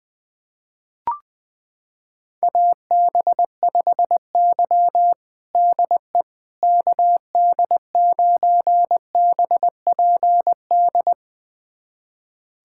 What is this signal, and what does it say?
Morse code, heard as a keyed tone, about 21 wpm, 700 Hz. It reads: AB5Y DE KD9BPD
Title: USA Callsign DE Callsign 20wpm